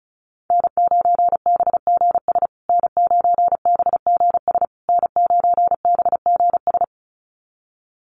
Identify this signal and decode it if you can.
Morse code, heard as a keyed tone, about 34 wpm, 700 Hz. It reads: N9BGS N9BGS N9BGS
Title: USA Call Signs Repeated 3x 35wpm